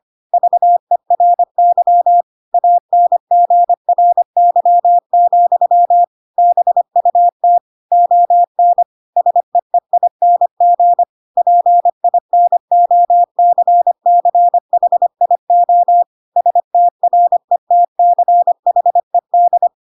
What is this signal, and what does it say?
Morse code, heard as a keyed tone, about 25 wpm, 700 Hz. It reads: VERY ANGRY, BUT ON SEEING PINOCCHIO STRETCHED
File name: Pinocchio / Pinocchio ch7 25 25 ik1hge